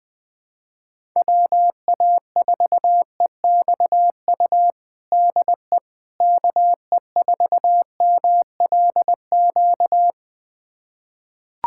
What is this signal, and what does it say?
Morse code, heard as a keyed tone, about 20 wpm, 700 Hz. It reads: WA4EXU DE KE4MLQ
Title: USA Callsign DE Callsign - 6 Characters - 20wpm